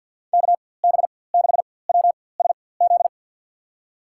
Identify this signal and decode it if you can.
Morse code, heard as a keyed tone, about 47 wpm, spaced as for 23 wpm, 700 Hz. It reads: KB6WSZ
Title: USA Call Signs - 6 Characters - 4x Character Spacing - 50wpm